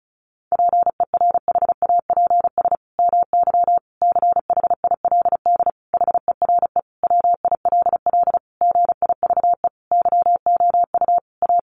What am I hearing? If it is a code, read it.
Morse code, heard as a keyed tone, about 34 wpm, 700 Hz. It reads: PERHAPS MY CHILD HERE WILL GIVE YOU A
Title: Sentences from Top 1100 Words 35wpm